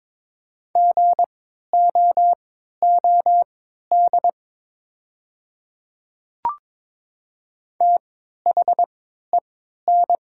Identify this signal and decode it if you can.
Morse code, heard as a keyed tone, about 22 wpm, spaced as for 13 wpm, 700 Hz. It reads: GOOD THEN
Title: 3x Character Spacing - Top 200 Words - 22wpm